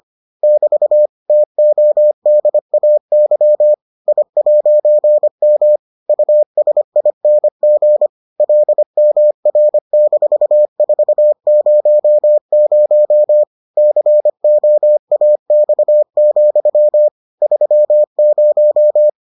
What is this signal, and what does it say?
Morse code, heard as a keyed tone, about 25 wpm, 600 Hz. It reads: <BT> TODAY I'M USING LMR-400 COAX, 30